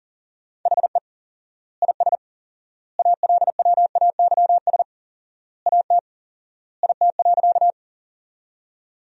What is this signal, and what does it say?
Morse code, heard as a keyed tone, about 37 wpm, 700 Hz. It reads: HE IS ALWAYS AT IT.
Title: 4x Word-Spacing - Sentences from Top 300 Words - 40wpm